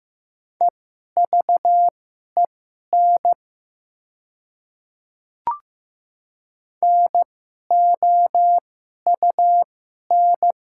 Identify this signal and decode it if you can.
Morse code, heard as a keyed tone, about 15 wpm, spaced as for 11 wpm, 700 Hz. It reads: EVEN NOUN
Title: 2x Character Spacing - Top 500 Words - 15wpm